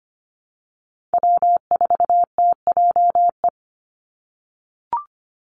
Morse code, heard as a keyed tone, about 25 wpm, 700 Hz. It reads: W4TJE
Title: Call Signs - 25wpm